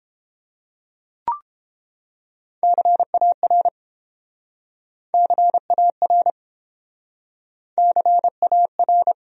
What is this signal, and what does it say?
Morse code, heard as a keyed tone, about 30 wpm, 700 Hz. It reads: CAR CAR CAR